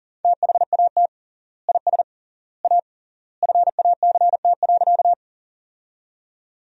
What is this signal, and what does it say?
Morse code, heard as a keyed tone, about 40 wpm, 700 Hz. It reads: THAT IS A FACT.
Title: 3x Word-Spacing - Sentences from Top 500 Words - 40wpm